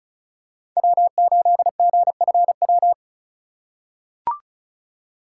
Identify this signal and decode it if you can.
Morse code, heard as a keyed tone, about 36 wpm, 690 Hz. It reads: W8GFW
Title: USA Call Signs - 5 Characters - 3x Character Spacing - 35wpm